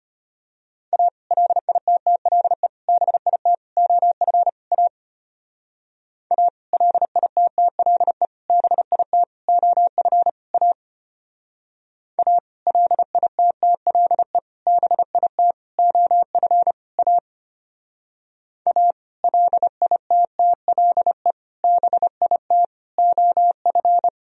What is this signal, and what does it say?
Morse code, heard as a keyed tone, about 32 wpm, 700 Hz. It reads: A LITTLE BIT OF A A LITTLE BIT OF A A LITTLE BIT OF A A LITTLE BIT OF